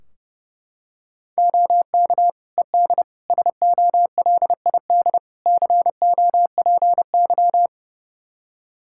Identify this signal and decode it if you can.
Morse code, heard as a keyed tone, about 30 wpm, 700 Hz. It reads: OK ED SOLID COPY
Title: K1USN SST QSO Practice 30wpm